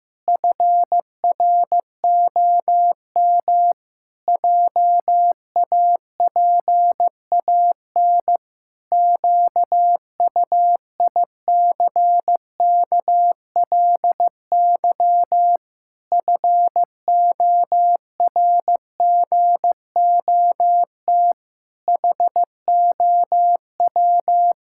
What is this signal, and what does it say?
Morse code, heard as a keyed tone, about 16 wpm, 700 Hz. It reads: FROM JAPAN QUICKLY FORGOT HOW
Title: Pangram Challenge v2 - 15wpm